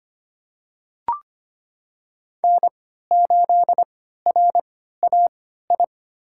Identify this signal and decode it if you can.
Morse code, heard as a keyed tone, about 25 wpm, spaced as for 14 wpm, 700 Hz. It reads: N8RAI
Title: USA Call Signs - 5 Characters - 3x Character Spacing - 25wpm